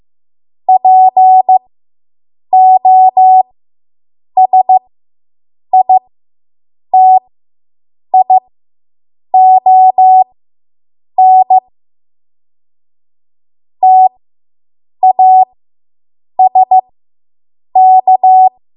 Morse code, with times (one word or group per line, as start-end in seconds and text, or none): 0.7-11.6 POSITION
13.8-18.5 TASK